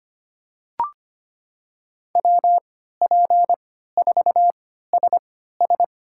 Morse code, heard as a keyed tone, about 25 wpm, spaced as for 14 wpm, 700 Hz. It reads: WP4SS
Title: USA Call Signs - 5 Characters - 3x Character Spacing - 25wpm